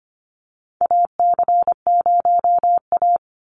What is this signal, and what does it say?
Morse code, heard as a keyed tone, about 25 wpm, 700 Hz. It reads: AC0A
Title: NA SOTA Chaser Callsigns 25wpm